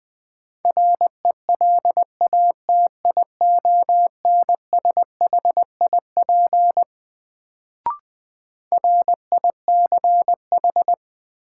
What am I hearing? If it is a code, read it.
Morse code, heard as a keyed tone, about 20 wpm, 700 Hz. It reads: RELATIONSHIP RICH
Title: Sentences from Top 1100 Words - New Word Drill Alt - 20wpm